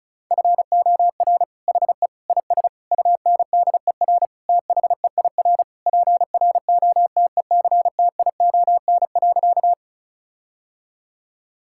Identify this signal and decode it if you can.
Morse code, heard as a keyed tone, about 34 wpm, 700 Hz. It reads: FOR HE IS UNDER THEIR PROTECTION.
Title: Sentences from Top 1600 Words 35wpm